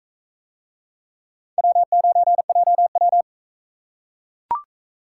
Morse code, heard as a keyed tone, about 38 wpm, 700 Hz. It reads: W9JW